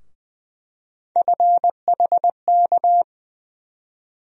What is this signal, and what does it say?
Morse code, heard as a keyed tone, about 20 wpm, 700 Hz. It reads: FHK
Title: USA Call Sign Postfixes - 3 Letters 20wpm